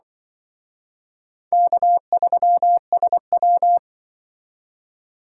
Morse code, heard as a keyed tone, about 24 wpm, 700 Hz. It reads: K3SW
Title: Speed Racing - USA 1x2 Call Signs 20wpm